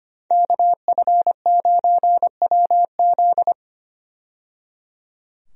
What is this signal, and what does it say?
Morse code, heard as a keyed tone, about 25 wpm, 700 Hz. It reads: KF9WZ
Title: USA Call Signs - 5 Characters - 25wpm